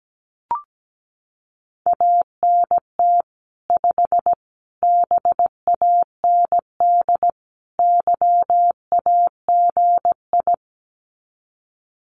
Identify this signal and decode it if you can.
Morse code, heard as a keyed tone, about 17 wpm, 700 Hz. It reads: ANT 5 BAND YAGI
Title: Common QSO Phrases 17wpm